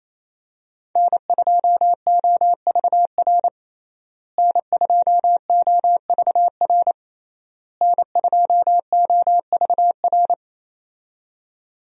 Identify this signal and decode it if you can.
Morse code, heard as a keyed tone, about 28 wpm, 700 Hz. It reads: N2OVR N2OVR N2OVR
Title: USA Call Signs - 5 Characters - Repeated 3x - 28wpm